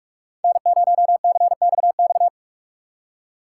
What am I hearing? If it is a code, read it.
Morse code, heard as a keyed tone, about 46 wpm, 700 Hz. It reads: N0CXX